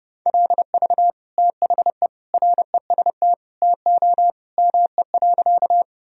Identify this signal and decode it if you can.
Morse code, heard as a keyed tone, about 32 wpm, 700 Hz. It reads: LV THE REST TO ME.
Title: Sentences from Top 500 Words and Abbreviations 30wpm